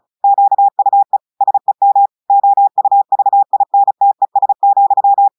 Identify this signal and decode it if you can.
Morse code, heard as a keyed tone, about 36 wpm, 800 Hz. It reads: QUE SEM OUVINTES,
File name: audios / portugais / portugais texte en clair 2 vitesse 35 mots minutes